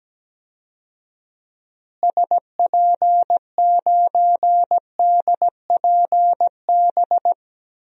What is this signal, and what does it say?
Morse code, heard as a keyed tone, about 17 wpm, 700 Hz. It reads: SP9DPB
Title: DXCC-100 Call Signs - 6 Characters - Repeated 3x - 17wpm